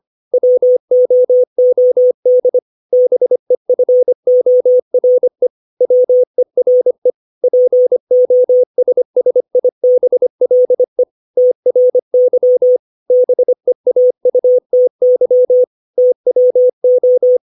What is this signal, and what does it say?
Morse code, heard as a keyed tone, about 25 wpm, 500 Hz. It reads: WOOD BEFORE WERE POSSIBLE TRY BEAUTY TWO
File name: 632 words / 632 Words 25WPM